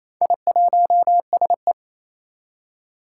Morse code, heard as a keyed tone, about 29 wpm, 700 Hz. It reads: I1SE
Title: Four Challenging Letter-Number Combinations 28wpm